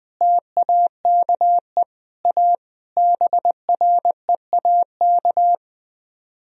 Morse code, heard as a keyed tone, about 20 wpm, 700 Hz. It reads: TAKE A BREAK
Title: Speed-Racing - Sets of 5 Words 17wpm